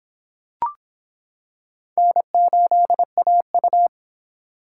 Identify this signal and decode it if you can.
Morse code, heard as a keyed tone, about 26 wpm, 700 Hz. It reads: N8AU